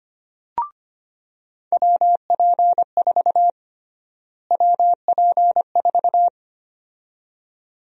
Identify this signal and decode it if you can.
Morse code, heard as a keyed tone, about 25 wpm, 700 Hz. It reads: WP4 WP4
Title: USA Call Sign Prefixes - Repeated 2x - 25wpm